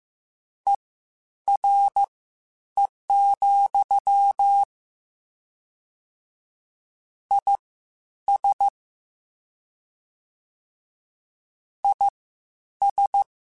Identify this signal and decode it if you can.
Morse code, heard as a keyed tone, about 15 wpm, spaced as for 9 wpm, 800 Hz. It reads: ERE, IS IS
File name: General / General-mp3s / Suffixes sp @13